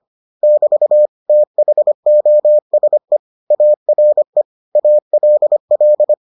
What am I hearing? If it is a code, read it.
Morse code, heard as a keyed tone, about 25 wpm, 600 Hz. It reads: <BT> THOSE ARE ALL